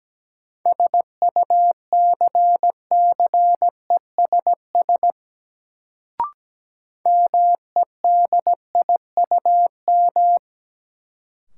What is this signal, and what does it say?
Morse code, heard as a keyed tone, about 17 wpm, 700 Hz. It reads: SUCCESS MEDIUM
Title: Sentences from Top 900 Words - New Word Drill - 17wpm